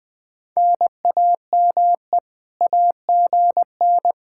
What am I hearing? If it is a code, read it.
Morse code, heard as a keyed tone, about 20 wpm, 700 Hz. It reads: NAME AGN